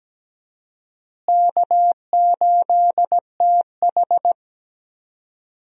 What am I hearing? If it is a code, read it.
Morse code, heard as a keyed tone, about 17 wpm, 700 Hz. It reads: K8TH
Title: USA Call Signs - 4 Characters - Repeated 3x - 17wpm